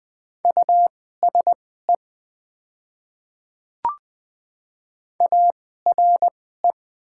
Morse code, heard as a keyed tone, about 20 wpm, spaced as for 14 wpm, 700 Hz. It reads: USE ARE
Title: 2x Character Spacing - Top 100 Words - 20wpm